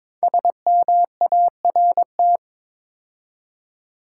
Morse code, heard as a keyed tone, about 22 wpm, 700 Hz. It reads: SMART
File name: 5 Letter Words - First 3 Letters Given - 22wpm